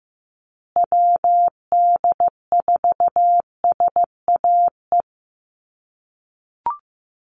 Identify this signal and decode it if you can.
Morse code, heard as a keyed tone, about 15 wpm, 700 Hz. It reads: WD4SAE
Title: USA Call Signs Repeated 3x 15wpm